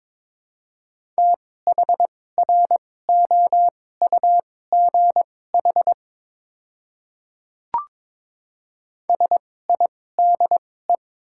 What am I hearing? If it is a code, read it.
Morse code, heard as a keyed tone, about 22 wpm, spaced as for 16 wpm, 700 Hz. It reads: THROUGH SIDE
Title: 2x Character Spacing - Top 300 Words - 22wpm